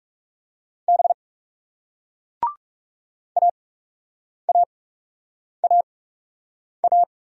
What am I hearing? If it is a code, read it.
Morse code, heard as a keyed tone, about 40 wpm, spaced as for 8 wpm, 700 Hz. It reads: B AAAA